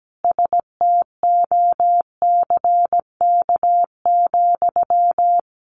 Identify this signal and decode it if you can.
Morse code, heard as a keyed tone, about 17 wpm, 700 Hz. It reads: STOCK,
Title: Sentences from Top 800 Words 17wpm